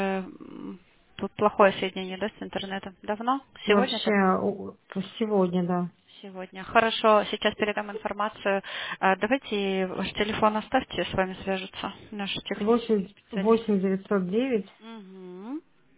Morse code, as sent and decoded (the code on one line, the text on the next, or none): none